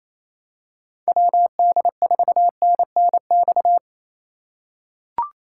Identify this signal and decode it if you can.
Morse code, heard as a keyed tone, about 28 wpm, 700 Hz. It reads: WD4NNX
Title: USA Call Signs - 6 Characters - 8x Character Spacing - 28wpm